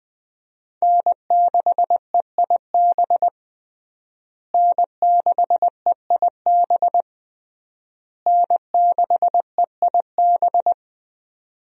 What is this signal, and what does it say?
Morse code, heard as a keyed tone, about 20 wpm, 700 Hz. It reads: N6EIB N6EIB N6EIB
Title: USA Call Signs - 5 Characters - Repeated 3x - 20wpm